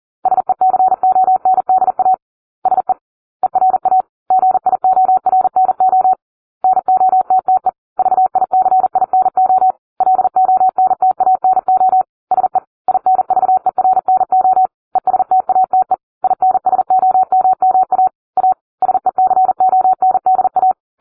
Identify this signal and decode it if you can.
Morse code, heard as a keyed tone, about 40 wpm, 750 Hz. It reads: SECONDA SE ERA GIORNO NOTTE VICINO LONTANO SE INVERNO ESTATE INSOMMA A SECONDA